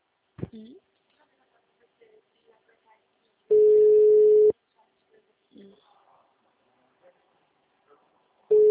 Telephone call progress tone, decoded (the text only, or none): ringback tone